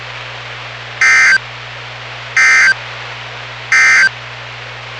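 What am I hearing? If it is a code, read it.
Emergency Alert System (SAME) tones, heard as harsh buzzing bursts. It is heard from 0.9 to 4.2 s.
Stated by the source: NOAA Weather Radio station WWG57 on 162.450 MHz, local weather broadcast.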